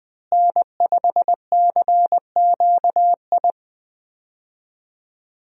Morse code, heard as a keyed tone, about 20 wpm, 700 Hz. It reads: N5CQI
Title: USA Call Signs - 5 Characters - Repeated 3x - 20wpm